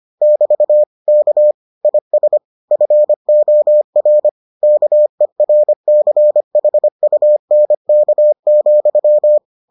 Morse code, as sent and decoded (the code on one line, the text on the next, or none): -...- / -.- / .. ... / ..-. --- .-. / -.- . .-. -.-. .... ..- -. -.- --..--
<BT> K IS FOR KERCHUNK,